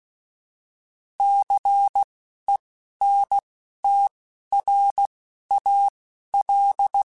Morse code, heard as a keyed tone, about 16 wpm, spaced as for 11 wpm, 800 Hz. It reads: CENTRAL